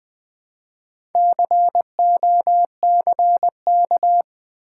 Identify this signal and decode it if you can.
Morse code, heard as a keyed tone, about 20 wpm, 700 Hz. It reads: COCK
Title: Build A Word - 2 Letters At A Time - 8-Letter Words - 20wpm